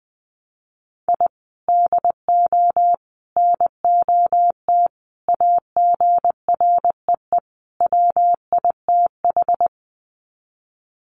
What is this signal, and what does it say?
Morse code, heard as a keyed tone, about 20 wpm, 700 Hz. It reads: I DO NOT AGREE WITH
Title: Sets of 5 Words Encore 20wpm